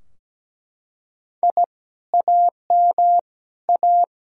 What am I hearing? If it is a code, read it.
Morse code, heard as a keyed tone, about 17 wpm, 700 Hz. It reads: I AM A
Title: Sentences from Top 1000 Words - Review Part 2 - 17wpm